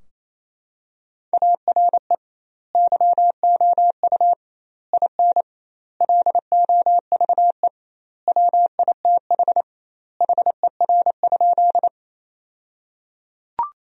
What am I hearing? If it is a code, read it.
Morse code, heard as a keyed tone, about 27 wpm, 700 Hz. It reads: ARE YOU IN LOVE WITH HER?
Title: 2x Word-Spacing - Sentences from Top 400 Words - 28wpm